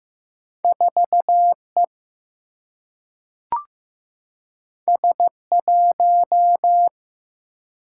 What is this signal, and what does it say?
Morse code, heard as a keyed tone, about 15 wpm, 700 Hz. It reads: 4 E S 1
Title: Single-Letter Single-Number 15wpm